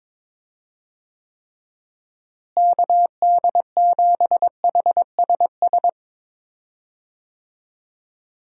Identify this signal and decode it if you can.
Morse code, heard as a keyed tone, about 22 wpm, 700 Hz. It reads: KD7HSS
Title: USA Call Signs - 6 Characters - Repeated 2x - 22wpm